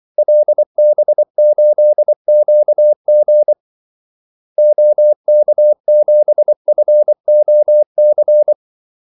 Morse code, heard as a keyed tone, about 24 wpm, 600 Hz. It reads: LB8QG OK7FOC
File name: Callsigns-24wpm